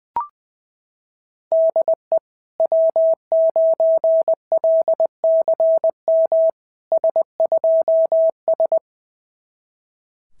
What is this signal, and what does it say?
Morse code, heard as a keyed tone, about 20 wpm, 660 Hz. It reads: DE W9LCM S2S
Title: SOTA QSO Practice 20wpm